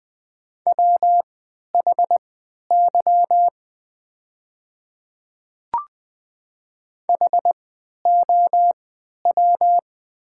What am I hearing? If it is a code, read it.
Morse code, heard as a keyed tone, about 20 wpm, spaced as for 11 wpm, 700 Hz. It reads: WHY HOW
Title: 3x Character Spacing - Top 300 Words - 20wpm